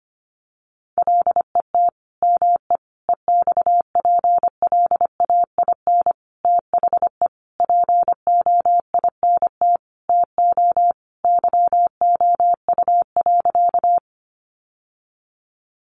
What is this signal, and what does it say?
Morse code, heard as a keyed tone, about 25 wpm, 700 Hz. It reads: LET ME EXPLAIN THE POINT TO YOU.
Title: Sentences from Top 1100 Words 25wpm